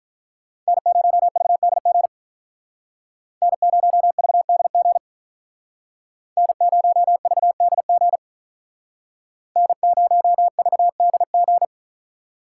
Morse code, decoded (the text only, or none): N0VDG N0VDG N0VDG N0VDG